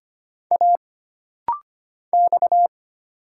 Morse code, heard as a keyed tone, about 25 wpm, 700 Hz. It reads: A X
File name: Single Character (Letter:Number:Punctuation:Prosign) - Rapid-Fire - 25wpm